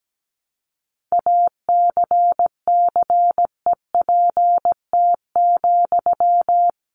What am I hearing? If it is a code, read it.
Morse code, heard as a keyed tone, about 17 wpm, 700 Hz. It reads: ACCEPT,
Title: Sentences from Top 1300 Words 17wpm